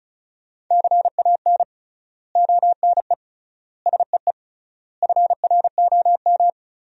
Morse code, heard as a keyed tone, about 36 wpm, 700 Hz. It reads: CAN ONE SEE FROM
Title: 3x Word-Spacing - Sentences from Top 100 Words - 35wpm